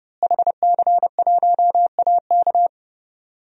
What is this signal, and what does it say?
Morse code, heard as a keyed tone, about 30 wpm, 700 Hz. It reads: HC1AK
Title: DXCC-100 Call Signs - 5 Characters - Repeated 2x - 30wpm